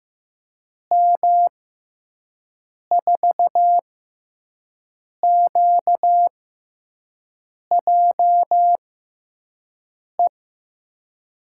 Morse code, heard as a keyed tone, about 15 wpm, spaced as for 5 wpm, 700 Hz. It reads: M4QJE
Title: USA Call Signs - 6 Characters - 6x Character Spacing - 15wpm